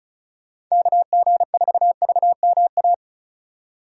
Morse code, heard as a keyed tone, about 36 wpm, 700 Hz. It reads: KG4VMA
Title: USA Call Signs - 6 Characters - 3x Character Spacing - 35wpm